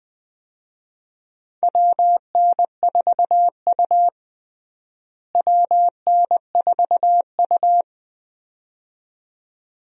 Morse code, heard as a keyed tone, about 21 wpm, 700 Hz. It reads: WN4U WN4U
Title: USA Call Signs - 4 Characters - Repeated 2x - 20wpm